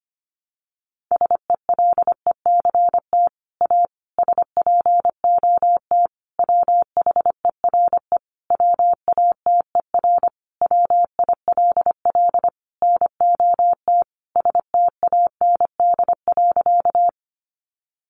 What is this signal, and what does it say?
Morse code, heard as a keyed tone, about 25 wpm, 700 Hz. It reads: SELECT A SPOT WHERE WATER WILL NOT STAND.
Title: Sentences from Top 1300 Words 25wpm